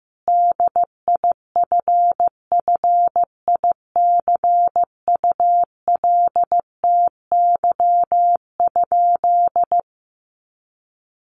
Morse code, heard as a keyed tone, about 15 wpm, 700 Hz. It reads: DIFFICULTY?
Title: Sentences from Top 1500 Words 15wpm